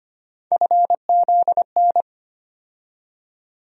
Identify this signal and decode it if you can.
Morse code, heard as a keyed tone, about 25 wpm, 700 Hz. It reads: FZN
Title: USA Call Sign Postfixes - 3 Letters 25wpm